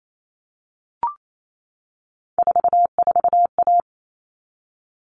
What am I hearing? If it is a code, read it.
Morse code, heard as a keyed tone, about 28 wpm, 700 Hz. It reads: 44A
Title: All Cut Numbers - 3-Digit 28wpm